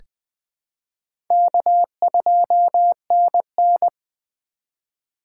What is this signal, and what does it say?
Morse code, heard as a keyed tone, about 21 wpm, 700 Hz. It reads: K2NN
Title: USA Call Signs - 4 Characters - 4x Character Spacing - 20wpm